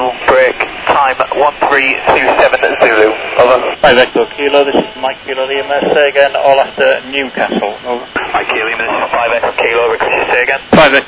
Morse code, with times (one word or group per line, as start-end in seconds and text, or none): none